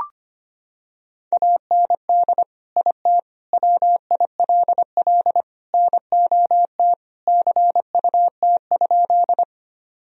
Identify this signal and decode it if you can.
Morse code, heard as a keyed tone, about 25 wpm, 700 Hz. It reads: AND IT WILL NOT CUT?
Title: Sentences from Top 1000 Words - Review Part 1 - 25wpm